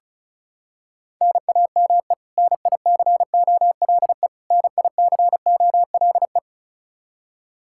Morse code, heard as a keyed tone, about 34 wpm, 700 Hz. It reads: NAME NICOLE NICOLE
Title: Practice QSO - Exchange 1 - Beginning - 35wpm